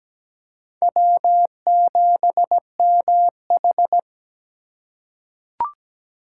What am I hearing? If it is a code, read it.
Morse code, heard as a keyed tone, about 17 wpm, 700 Hz. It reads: W7MH
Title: USA Call Signs - 4 Characters - 2x Character Spacing - 17wpm